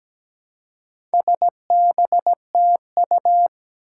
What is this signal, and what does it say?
Morse code, heard as a keyed tone, about 17 wpm, 700 Hz. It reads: SBTU
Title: Four Challenging Letter-Number Combinations 17wpm